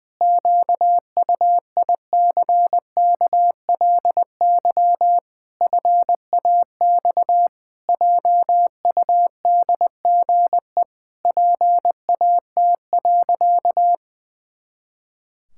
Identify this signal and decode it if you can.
Morse code, heard as a keyed tone, about 20 wpm, 700 Hz. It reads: QUICKLY FAX JUDGE PAT.
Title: Pangram Challenge v2 - 20wpm